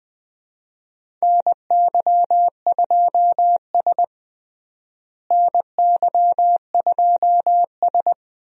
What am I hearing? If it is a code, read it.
Morse code, heard as a keyed tone, about 20 wpm, 700 Hz. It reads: NY2S NY2S
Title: USA Call Signs - 4 Characters - Repeated 2x - 20wpm